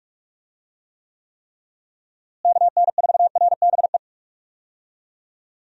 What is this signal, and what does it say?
Morse code, heard as a keyed tone, about 41 wpm, 700 Hz. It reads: KN4RBE
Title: USA Call Signs - 6 Characters - Repeated 3x - 45wpm